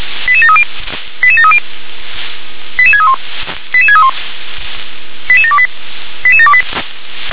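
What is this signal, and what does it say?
Five-tone selective call (ZVEI), heard as quick runs of five tones, tones of about 70 ms. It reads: 88730 88730 88631 88631 88728 88728
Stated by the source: ZVEI five-tone selective call received on an SDR.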